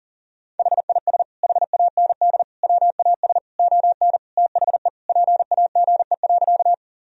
Morse code, heard as a keyed tone, about 37 wpm, 700 Hz. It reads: HIS HAND WAS ON THE PAGE.
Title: Sentences from Top 1000 Words - Review Part 1 - 40wpm